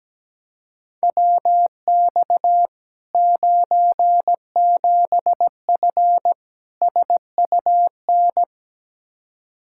Morse code, heard as a keyed tone, about 17 wpm, 700 Hz. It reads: WX 97F SUN
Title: Practice QSO - Exchange 2 - Info - 17wpm